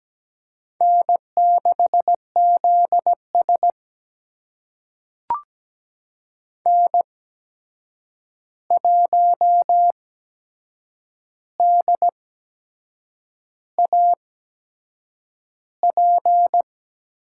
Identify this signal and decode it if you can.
Morse code, heard as a keyed tone, about 17 wpm, spaced as for 7 wpm, 700 Hz. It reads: N6ZS N1DAP
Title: USA Call Signs - 8x Character Spacing - 17wpm